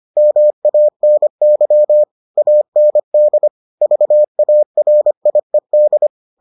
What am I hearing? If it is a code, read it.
Morse code, heard as a keyed tone, about 25 wpm, 600 Hz. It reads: MANY AND VARIED